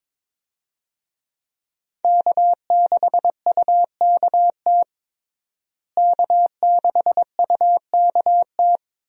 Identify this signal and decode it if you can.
Morse code, heard as a keyed tone, about 22 wpm, 700 Hz. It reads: K6UKT K6UKT
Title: USA Call Signs - 5 Characters - Repeated 3x - 22wpm